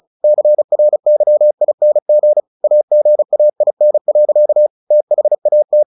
Morse code, heard as a keyed tone, about 35 wpm, 600 Hz. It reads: CRYING AGAIN. THAT
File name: the old man and the sea 2 35wpm